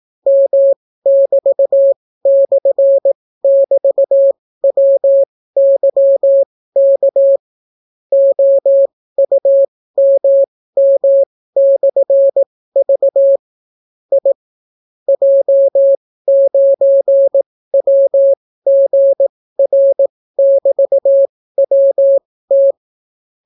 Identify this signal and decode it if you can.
Morse code, heard as a keyed tone, about 18 wpm, spaced as for 15 wpm, 550 Hz. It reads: M<BT>/<BT>WYK OUMM/V I J9WGR<BT>WT